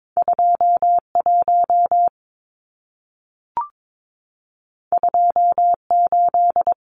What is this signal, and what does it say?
Morse code, heard as a keyed tone, about 22 wpm, 700 Hz. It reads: 21 28
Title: Numbers - 2-Digit 22wpm